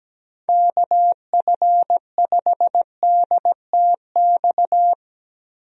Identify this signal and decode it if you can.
Morse code, heard as a keyed tone, about 17 wpm, 700 Hz. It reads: KF5DTX
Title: USA Call Signs - 6 Characters - Repeated 3x - 17wpm